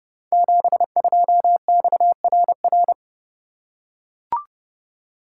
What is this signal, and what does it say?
Morse code, heard as a keyed tone, about 30 wpm, 700 Hz. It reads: 72XRR
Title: Code Groups - 5 Letter:Numbers - 30wpm